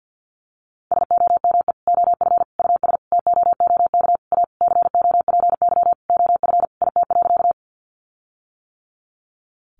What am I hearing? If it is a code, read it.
Morse code, heard as a keyed tone, about 48 wpm, 700 Hz. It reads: SOME OF US TOOK A COPY OF IT.
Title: Sentences from Top 1400 Words 50wpm